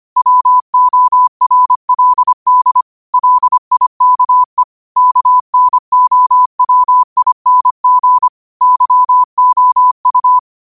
Morse code, heard as a keyed tone, about 25 wpm, 1000 Hz. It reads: WORLD LIKE KNOWING YOU
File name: qotdfiles / 25wpm